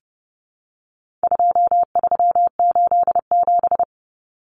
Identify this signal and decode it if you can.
Morse code, heard as a keyed tone, about 28 wpm, 700 Hz. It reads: 2387
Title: Common Cut Numbers - 4-Digit 30wpm